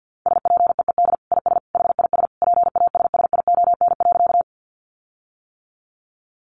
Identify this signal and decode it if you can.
Morse code, heard as a keyed tone, about 50 wpm, 700 Hz. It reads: SPEED IS HIS PASSION.
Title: Sentences from Top 1300 Words 50wpm